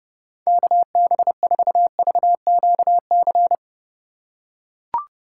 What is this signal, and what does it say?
Morse code, heard as a keyed tone, about 30 wpm, 700 Hz. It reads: KB4VQC